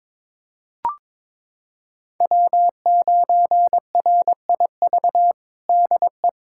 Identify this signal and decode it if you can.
Morse code, heard as a keyed tone, about 21 wpm, 700 Hz. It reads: W9RIV DE
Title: USA Callsign DE Callsign - 5 Characters - 22wpm